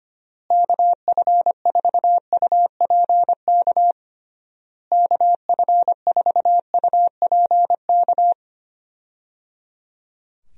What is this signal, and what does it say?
Morse code, heard as a keyed tone, about 25 wpm, 700 Hz. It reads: KF4UPK KF4UPK
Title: USA Call Signs - 6 Characters - Repeated 3x - 25wpm